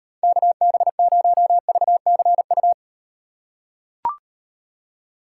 Morse code, heard as a keyed tone, about 36 wpm, 700 Hz. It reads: KB0VCU